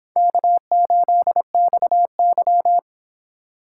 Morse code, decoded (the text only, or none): K8XY